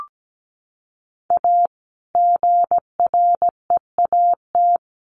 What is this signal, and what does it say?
Morse code, heard as a keyed tone, about 17 wpm, 700 Hz. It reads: A GREAT
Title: Sets of 3 Words 17wpm